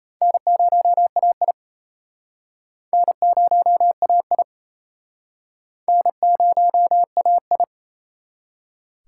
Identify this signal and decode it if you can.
Morse code, heard as a keyed tone, about 37 wpm, 700 Hz. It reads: N0AI N0AI N0AI